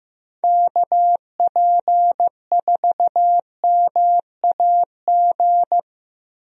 Morse code, heard as a keyed tone, about 15 wpm, 700 Hz. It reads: KP4MAG
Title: USA Call Signs - 6 Characters - Repeated 3x - 15wpm